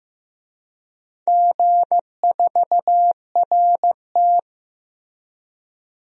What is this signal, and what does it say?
Morse code, heard as a keyed tone, about 15 wpm, 700 Hz. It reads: G4RT
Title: DXCC-100 Call Signs - 4 Characters - 15wpm